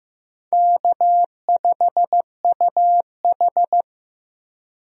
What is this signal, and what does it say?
Morse code, heard as a keyed tone, about 15 wpm, 700 Hz. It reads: K5UH
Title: USA Call Signs - 4 Characters - Repeated 2x - 15wpm